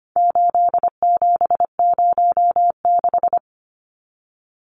Morse code, heard as a keyed tone, about 25 wpm, 700 Hz. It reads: 8706
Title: Numbers - 4-Digit 25wpm